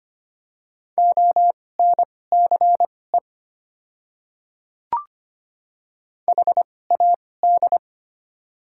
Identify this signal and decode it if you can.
Morse code, heard as a keyed tone, about 25 wpm, spaced as for 18 wpm, 700 Hz. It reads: ONCE HAD
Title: 2x Character Spacing - Top 300 Words - 25wpm